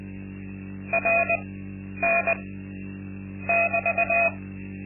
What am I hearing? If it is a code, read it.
Morse code, heard as a keyed tone, about 20 wpm, spaced as for 10 wpm, 680 Hz. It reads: RN <BT>
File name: Linksmakalnis-2